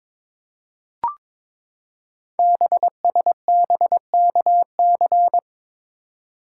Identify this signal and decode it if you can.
Morse code, heard as a keyed tone, about 22 wpm, 700 Hz. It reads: BSBKC
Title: Code Groups - 5 Letters - 22wpm